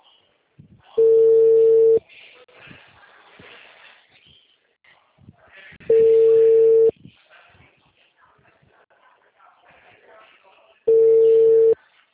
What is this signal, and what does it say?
Ringback tone (China, 450 Hz), with speech.
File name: out- -2011- - - .6487